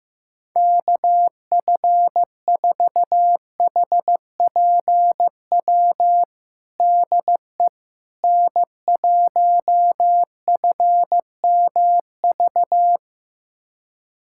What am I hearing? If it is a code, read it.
Morse code, heard as a keyed tone, about 15 wpm, 700 Hz. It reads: KF4HPW DE N1FMV
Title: USA Callsign DE Callsign 15wpm